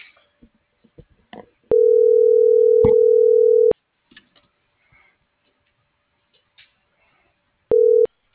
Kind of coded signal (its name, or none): telephone call progress tone